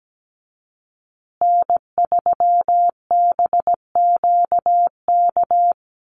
Morse code, heard as a keyed tone, about 17 wpm, 700 Hz. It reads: N3BQK
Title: USA Call Signs Repeated 2x 17wpm